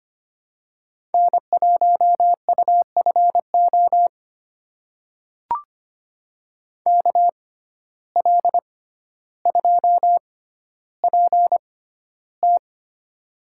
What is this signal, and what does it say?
Morse code, heard as a keyed tone, about 25 wpm, spaced as for 13 wpm, 700 Hz. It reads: N1UFO KL2PT
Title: USA Call Signs - 5 Characters - 6x Character Spacing - 25wpm